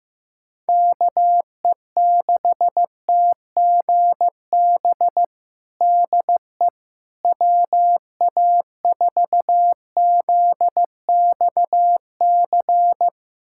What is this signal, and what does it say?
Morse code, heard as a keyed tone, about 15 wpm, 700 Hz. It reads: KE6TGB DE WA4ZXC
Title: USA Callsign DE Callsign 15wpm